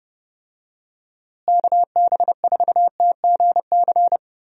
Morse code, heard as a keyed tone, about 32 wpm, 700 Hz. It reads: KB4TGC
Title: USA Call Signs - 6 Characters - Repeated 2x - 30wpm